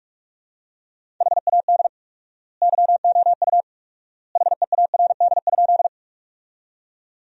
Morse code, heard as a keyed tone, about 46 wpm, 700 Hz. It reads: HAD YOU HEARD?
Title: 4x Word-Spacing - Sentences from Top 400 Words - 45wpm